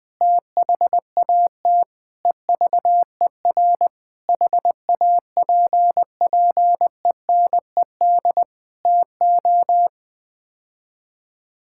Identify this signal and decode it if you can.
Morse code, heard as a keyed tone, about 21 wpm, 700 Hz. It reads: THAT EVER HAPPENED TO
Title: Sets of 8 Words - 20wpm